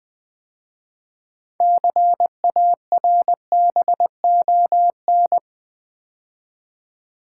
Morse code, heard as a keyed tone, about 20 wpm, 700 Hz. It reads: CARBON
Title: 6 Letter Words - First 3 Letters Given - 20wpm